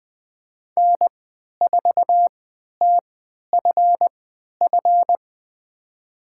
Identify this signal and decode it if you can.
Morse code, heard as a keyed tone, about 20 wpm, spaced as for 11 wpm, 700 Hz. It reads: N4TFF
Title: USA Call Signs - 5 Characters - 3x Character Spacing - 20wpm